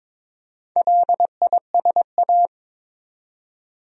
Morse code, heard as a keyed tone, about 22 wpm, 700 Hz. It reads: LISA
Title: USA Common Names - Max 4 Letters - 22wpm